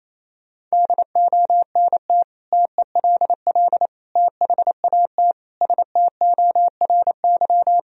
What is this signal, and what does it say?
Morse code, heard as a keyed tone, about 28 wpm, 700 Hz. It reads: DONT TELL THAT STORY
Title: Sentences from Top 1000 Words - Review Part 1 - 28wpm